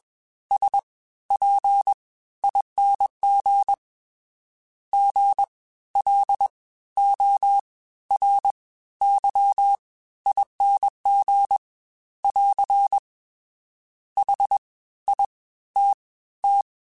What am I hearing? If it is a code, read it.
Morse code, heard as a keyed tone, about 21 wpm, spaced as for 14 wpm, 800 Hz. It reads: SPING GLORYING <AR> HITT